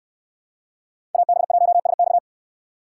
Morse code, heard as a keyed tone, about 69 wpm, 700 Hz. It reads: AB1SX